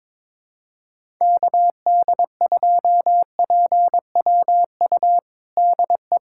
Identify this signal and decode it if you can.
Morse code, heard as a keyed tone, about 22 wpm, 700 Hz. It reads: KD2PWU DE
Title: USA Callsign DE Callsign 22wpm